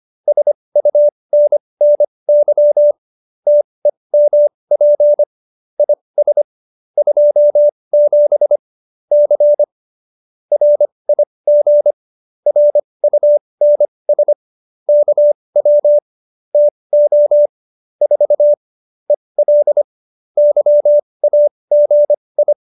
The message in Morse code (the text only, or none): SUNNY TEMP IS 27 C RIG RUNS KW TO 4 EL YAGI